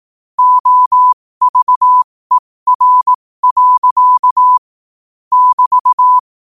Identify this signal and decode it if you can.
Morse code, heard as a keyed tone, about 18 wpm, spaced as for 15 wpm, 1000 Hz. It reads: OVER. <BT>
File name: qotdfiles / 15wpm